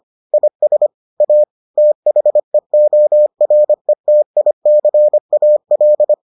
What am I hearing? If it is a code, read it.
Morse code, heard as a keyed tone, about 25 wpm, 600 Hz. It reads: IS A THEORETICAL